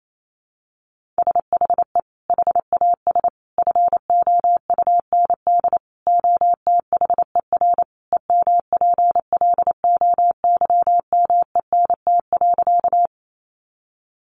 Morse code, heard as a keyed tone, about 28 wpm, 700 Hz. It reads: SHE HAS FOUND OTHER EMPLOYMENT.
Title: Sentences from Top 1500 Words 28wpm